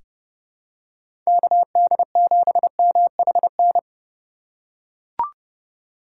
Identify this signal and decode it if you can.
Morse code, heard as a keyed tone, about 32 wpm, 700 Hz. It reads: KD7MHN
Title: USA Call Signs - 6 Characters - 2x Character Spacing - 30wpm